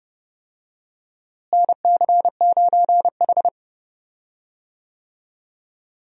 Morse code, heard as a keyed tone, about 30 wpm, 700 Hz. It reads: NC9H
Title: USA Call Signs - 4 Characters - Repeated 3x - 30wpm